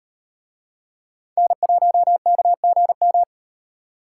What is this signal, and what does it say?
Morse code, heard as a keyed tone, about 38 wpm, 700 Hz. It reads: N1KGM